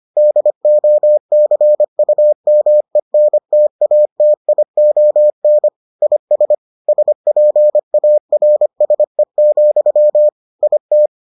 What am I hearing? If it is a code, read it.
Morse code, heard as a keyed tone, about 25 wpm, 600 Hz. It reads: DOCUMENTATION IS SPARSE, IT